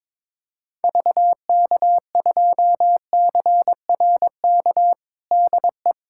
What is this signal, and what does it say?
Morse code, heard as a keyed tone, about 22 wpm, 700 Hz. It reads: VK2CRK DE
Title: USA Callsign DE Callsign 22wpm